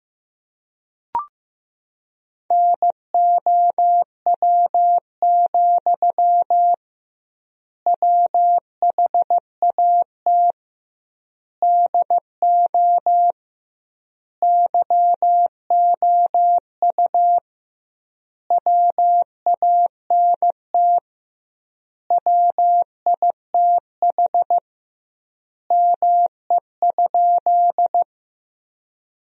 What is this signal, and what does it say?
Morse code, heard as a keyed tone, about 15 wpm, 700 Hz. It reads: NOW, WHAT DO YOU WANT WITH ME?
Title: 2x Word-Spacing - Sentences from Top 200 Words - 15wpm